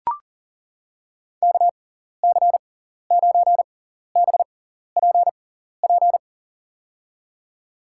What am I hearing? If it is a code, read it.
Morse code, heard as a keyed tone, about 40 wpm, spaced as for 14 wpm, 700 Hz. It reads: KC9BPP
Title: USA Call Signs - 6 Characters - 6x Character Spacing - 40wpm